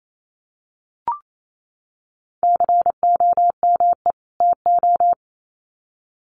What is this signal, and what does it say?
Morse code, heard as a keyed tone, about 28 wpm, 700 Hz. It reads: COME TO
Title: Sets of 2 Words 28wpm